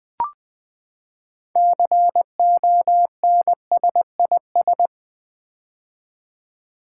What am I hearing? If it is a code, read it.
Morse code, heard as a keyed tone, about 20 wpm, 700 Hz. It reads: CONSIS
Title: Build A Word - 6 Letters At A Time - 12-Letter Words - 20wpm